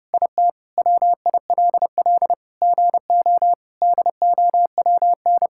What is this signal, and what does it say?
Morse code, heard as a keyed tone, about 30 wpm, 700 Hz. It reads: IT WILL GO DOWN
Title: Sentences from Top 1000 Words - Review Part 5 - 30wpm